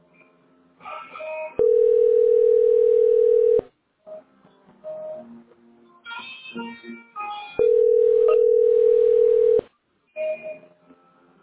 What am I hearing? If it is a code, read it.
Ringback tone (North America), 440 + 480 Hz.